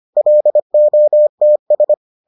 Morse code, heard as a keyed tone, about 25 wpm, 600 Hz. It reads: LOTS